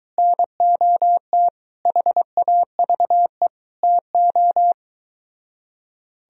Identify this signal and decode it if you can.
Morse code, heard as a keyed tone, about 22 wpm, 700 Hz. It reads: NOT HAVE TO